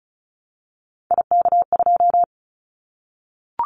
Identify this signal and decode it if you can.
Morse code, heard as a keyed tone, about 34 wpm, 700 Hz. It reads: IK2
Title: Callsign Prefixes for Top 25 DXCC - Repeated 3x - 35wpm